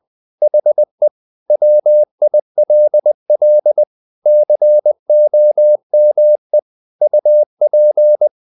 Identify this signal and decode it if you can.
Morse code, heard as a keyed tone, about 20 wpm, 600 Hz. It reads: HE WILL COME UP
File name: the old man and the sea / the old man and the sea part 1 20wpm